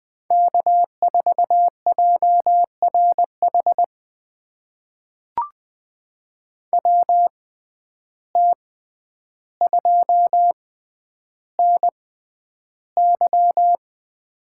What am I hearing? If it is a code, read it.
Morse code, heard as a keyed tone, about 21 wpm, spaced as for 10 wpm, 700 Hz. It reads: K4JRH WT2NY
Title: USA Call Signs - 5 Characters - 6x Character Spacing - 20wpm